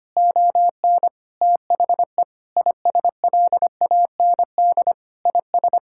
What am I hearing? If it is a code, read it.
Morse code, heard as a keyed tone, about 25 wpm, 700 Hz. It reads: ON THE ISLAND IS
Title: Sentences from Top 1000 Words - Review Part 6 - 25wpm